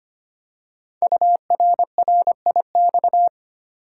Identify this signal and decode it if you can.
Morse code, heard as a keyed tone, about 25 wpm, 700 Hz. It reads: URRIX
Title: Code Groups - 5 Letters - 25wpm